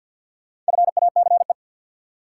Morse code, heard as a keyed tone, about 47 wpm, 700 Hz. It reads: FACE
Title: Four Letter Words - 50wpm